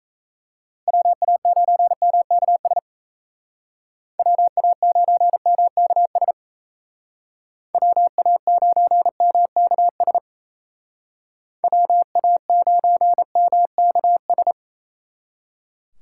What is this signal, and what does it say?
Morse code, heard as a keyed tone, about 33 wpm, 700 Hz. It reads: WA9MKS WA9MKS WA9MKS WA9MKS
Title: Speed Racing - USA 2x3 Call Signs 28wpm